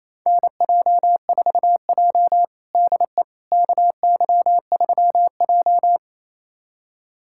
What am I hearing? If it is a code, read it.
Morse code, heard as a keyed tone, about 28 wpm, 700 Hz. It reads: NJ4J DE KY3J
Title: USA Callsign DE Callsign - 4 Characters - 28wpm